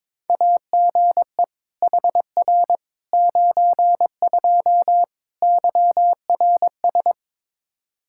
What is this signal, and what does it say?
Morse code, heard as a keyed tone, about 22 wpm, 700 Hz. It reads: AGE HR 92 YRS
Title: Practice QSO - Exchange 3 - More Info - 22wpm